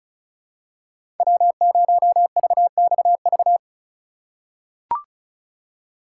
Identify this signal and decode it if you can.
Morse code, heard as a keyed tone, about 36 wpm, 700 Hz. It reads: W0VXV
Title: USA Call Signs - 4x Character Spacing - 35wpm